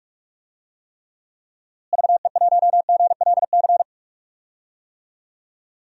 Morse code, heard as a keyed tone, about 41 wpm, 700 Hz. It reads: VE1GLC
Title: DXCC-100 Call Signs - 6 Characters - Repeated 2x - 45wpm